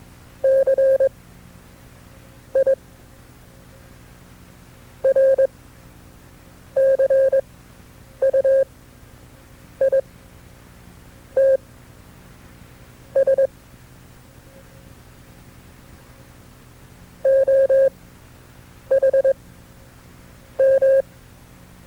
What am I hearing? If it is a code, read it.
Morse code, heard as a keyed tone, about 21 wpm, spaced as for 6 wpm, 540 Hz. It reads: CIRCUITS OHM
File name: FreeFiles / CW-practice / KB6MT-tape2-side2